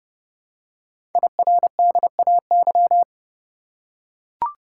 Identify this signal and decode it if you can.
Morse code, heard as a keyed tone, about 29 wpm, 700 Hz. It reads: IRDAY
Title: Code Groups - 5 Letters - 30wpm